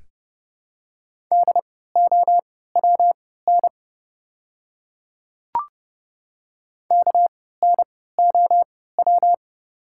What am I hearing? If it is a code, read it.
Morse code, heard as a keyed tone, about 30 wpm, spaced as for 17 wpm, 700 Hz. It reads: DOWN KNOW
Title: 3x Character Spacing - Top 100 Words - 30wpm